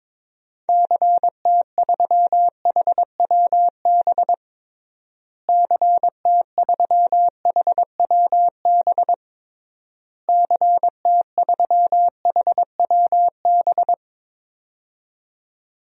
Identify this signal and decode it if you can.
Morse code, heard as a keyed tone, about 22 wpm, 700 Hz. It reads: CT3HWB CT3HWB CT3HWB
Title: DXCC-100 Call Signs - 6 Characters - Repeated 3x - 22wpm